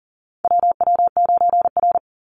Morse code, heard as a keyed tone, about 40 wpm, 700 Hz. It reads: WW9R
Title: Call Signs Repeated 3x - 40wpm